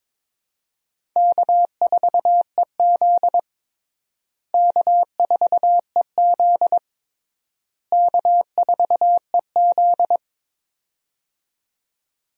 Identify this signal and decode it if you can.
Morse code, heard as a keyed tone, about 22 wpm, 700 Hz. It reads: K4EZ K4EZ K4EZ
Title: USA Call Signs - 4 Characters - Repeated 3x - 22wpm